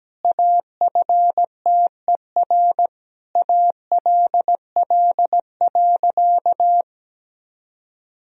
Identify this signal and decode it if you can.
Morse code, heard as a keyed tone, about 17 wpm, 700 Hz. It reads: AFTER ALL.
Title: Sentences from Top 1000 Words - Review Part 3 - 17wpm